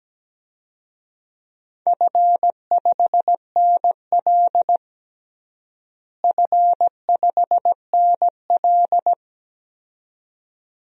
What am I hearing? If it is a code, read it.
Morse code, heard as a keyed tone, about 17 wpm, 700 Hz. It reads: F5NL F5NL
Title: DXCC-100 Call Signs - 4 Characters - Repeated 2x - 17wpm